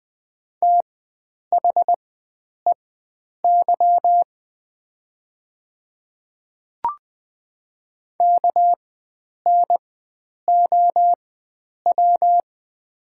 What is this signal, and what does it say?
Morse code, heard as a keyed tone, about 20 wpm, spaced as for 9 wpm, 700 Hz. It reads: THEY KNOW
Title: 4x Character Spacing - Top 300 Words - 20wpm